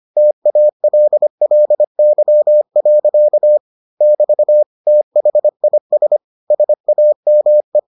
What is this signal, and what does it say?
Morse code, heard as a keyed tone, about 25 wpm, 600 Hz. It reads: TALLY. <BT> THIS SAME